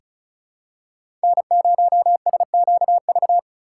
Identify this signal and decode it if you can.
Morse code, heard as a keyed tone, about 36 wpm, 700 Hz. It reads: N0SQV
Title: USA Call Signs - 2x Character Spacing - 35wpm